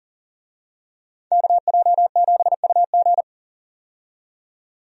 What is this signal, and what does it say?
Morse code, heard as a keyed tone, about 40 wpm, 700 Hz. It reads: KJ7UG
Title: USA Call Signs - 5 Characters - 40wpm